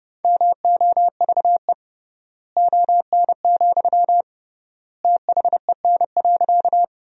Morse code, heard as a keyed tone, about 32 wpm, 700 Hz. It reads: MOVE ON, THEN.
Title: 3x Word-Spacing - Sentences from Top 200 Words - 30wpm